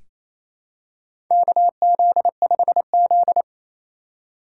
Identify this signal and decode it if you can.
Morse code, heard as a keyed tone, about 28 wpm, 700 Hz. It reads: KZ5Z
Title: K1USN SST QSO Practice - 28wpm